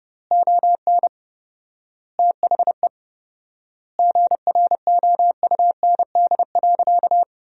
Morse code, heard as a keyed tone, about 30 wpm, 700 Hz. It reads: ON THE GROUND.
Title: 4x Word-Spacing - Sentences from Top 500 Words - 30wpm